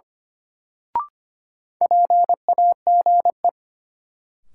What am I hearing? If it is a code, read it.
Morse code, heard as a keyed tone, about 25 wpm, 700 Hz. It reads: PAGE
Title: Sentences from Top 1000 Words - Review Part 1 - New Word Drill - 25wpm